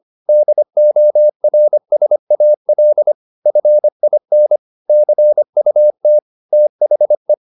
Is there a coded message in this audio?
Yes: Morse code from 0.3 to 7.3 s, heard as a keyed tone, about 25 wpm, 600 Hz. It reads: DORSAL FIN CUT THE